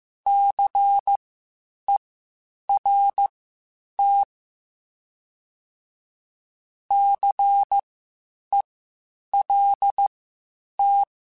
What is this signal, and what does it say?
Morse code, heard as a keyed tone, about 15 wpm, spaced as for 8 wpm, 800 Hz. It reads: CERT CELT
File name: Spaced Alphas / Spaced Alphas mp3s / Cs sp @13